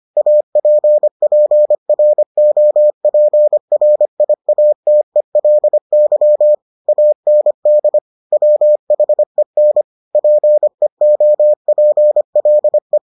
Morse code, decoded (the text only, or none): APPROPRIATELY AND WHEN PEOPLE